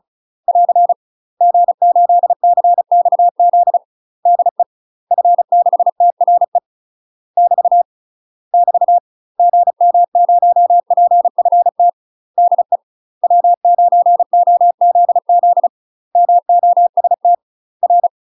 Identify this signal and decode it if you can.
Morse code, heard as a keyed tone, about 35 wpm, 700 Hz. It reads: <AR> G8CXZ DE F6TRE <BT> <BT> GM0PFT DE W9OZZ MOST R
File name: UK MASTER QSOs TRAINING / 35WPM / UK MASTER QSOs 0025